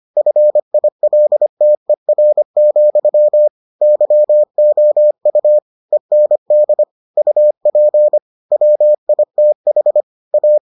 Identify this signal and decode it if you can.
Morse code, heard as a keyed tone, about 25 wpm, 600 Hz. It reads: FILTER, YOU END UP WITH A